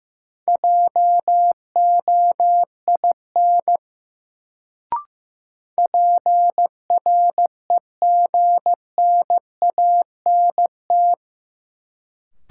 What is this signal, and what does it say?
Morse code, heard as a keyed tone, about 15 wpm, 700 Hz. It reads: JOIN PREGNANT
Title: Sentences from Top 1000 Words - Review Part 6 - New Word Drill - 15wpm